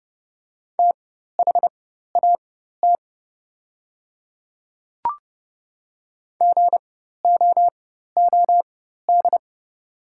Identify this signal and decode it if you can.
Morse code, heard as a keyed tone, about 30 wpm, spaced as for 14 wpm, 700 Hz. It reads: THAT GOOD
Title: 4x Character Spacing - Top 300 Words - 30wpm